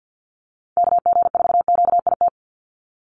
Morse code, heard as a keyed tone, about 50 wpm, 700 Hz. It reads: KG4QIT
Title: USA Call Signs 50wpm